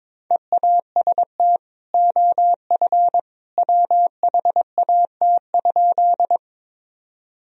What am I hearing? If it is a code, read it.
Morse code, heard as a keyed tone, about 22 wpm, 700 Hz. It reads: EAST OF WHAT?
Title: Sentences from Top 1000 Words - Review Part 3 - 22wpm